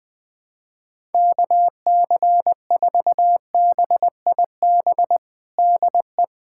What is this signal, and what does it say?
Morse code, heard as a keyed tone, about 21 wpm, 700 Hz. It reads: KC4BIB DE
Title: USA Callsign DE Callsign - 6 Characters - 20wpm